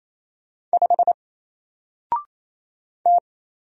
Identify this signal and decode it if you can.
Morse code, heard as a keyed tone, about 27 wpm, 700 Hz. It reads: 5 T
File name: Single-Letter-Number-28wpm